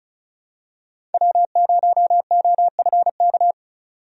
Morse code, heard as a keyed tone, about 36 wpm, 700 Hz. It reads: W0OFK
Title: USA Call Signs - 5 Characters - 35wpm